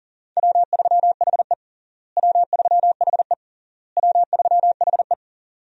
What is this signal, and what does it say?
Morse code, heard as a keyed tone, about 40 wpm, 700 Hz. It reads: W3HE W3HE W3HE
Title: USA Call Signs - 4 Characters - Repeated 3x - 40wpm